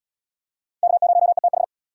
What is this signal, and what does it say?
Morse code, heard as a keyed tone, about 68 wpm, 700 Hz. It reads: K0ETH